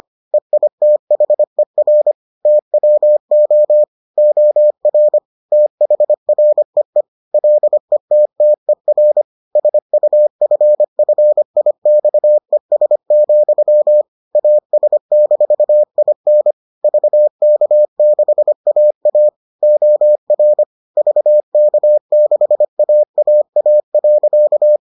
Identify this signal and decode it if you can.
Morse code, heard as a keyed tone, about 25 wpm, 600 Hz. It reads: EITHER TWO OR THREE LETTER SUFFIXES, AS-IN VK6AA OR VK6AAA.